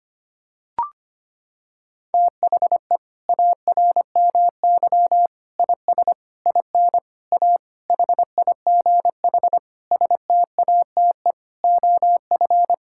Morse code, heard as a keyed tone, about 25 wpm, 700 Hz. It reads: THE ARMY IS IN A HIGH STATE OF
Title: Sentences from Top 1000 Words - Review Part 6 - 25wpm